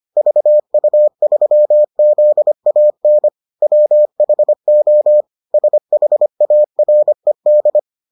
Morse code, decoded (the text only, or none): VU3ZAN WHO SHARED